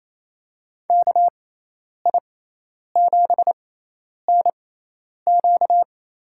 Morse code, heard as a keyed tone, about 28 wpm, spaced as for 10 wpm, 700 Hz. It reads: KI7NQ
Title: USA Call Signs - 6 Characters - 6x Character Spacing - 28wpm